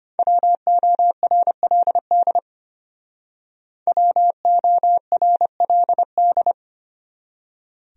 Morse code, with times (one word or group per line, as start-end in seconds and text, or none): 0.2-2.4 WORLD
3.9-6.5 WORLD